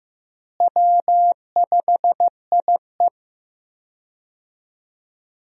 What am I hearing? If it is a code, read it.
Morse code, heard as a keyed tone, about 15 wpm, 700 Hz. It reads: W5IE
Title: USA Call Signs - 4 Characters - Repeated 3x - 15wpm